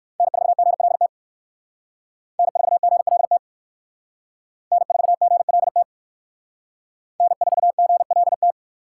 Morse code, heard as a keyed tone, about 53 wpm, 700 Hz. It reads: N4GLT N4GLT N4GLT N4GLT
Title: Speed Racing - USA 1x3 Call Signs 45wpm